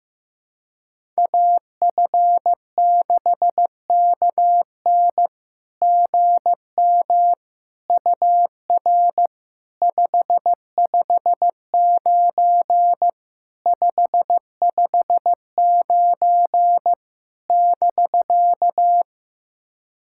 Morse code, heard as a keyed tone, about 16 wpm, 700 Hz. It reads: AF6KN GM UR 559 559 <BK>
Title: SOTA QSO Practice 15wpm